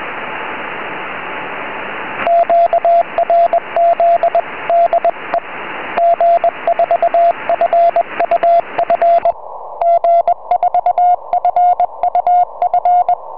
Morse code, heard as a keyed tone, about 21 wpm, 720 Hz, with radio noise. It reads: QRZ DE G4FUF G4FUF
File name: g4fuf 4m